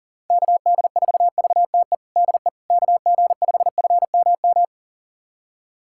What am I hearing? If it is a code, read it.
Morse code, heard as a keyed tone, about 40 wpm, 700 Hz. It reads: KD4VTE DE KG5FMM
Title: USA Callsign DE Callsign - 6 Characters - 40wpm